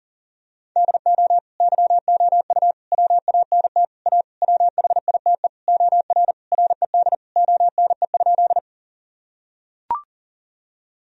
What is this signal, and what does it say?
Morse code, heard as a keyed tone, about 40 wpm, 700 Hz. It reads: DO YOU WANT A WHITE OR RED ONE?
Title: Sentences from Top 1000 Words - Review Part 2 - 40wpm